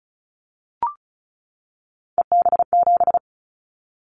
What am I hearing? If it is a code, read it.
Morse code, heard as a keyed tone, about 34 wpm, 700 Hz. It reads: EB7
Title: Callsign Prefixes for Top 25 DXCC - 35wpm